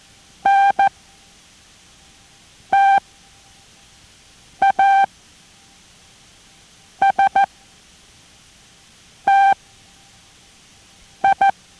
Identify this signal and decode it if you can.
Morse code, heard as a keyed tone, about 14 wpm, spaced as for 4 wpm, 790 Hz. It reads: NTASTI